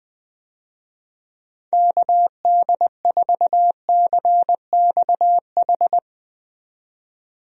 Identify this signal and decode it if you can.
Morse code, heard as a keyed tone, about 20 wpm, 700 Hz. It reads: KD4CXH
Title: USA Call Signs - 6 Characters - 20wpm